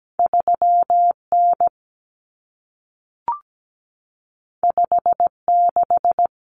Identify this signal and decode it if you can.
Morse code, heard as a keyed tone, about 17 wpm, 700 Hz. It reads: 3N 56
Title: Common Cut Numbers - 2-Digit 17wpm